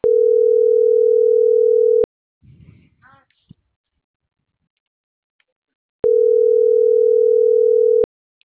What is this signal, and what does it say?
Ringback tone (North America), 440 + 480 Hz.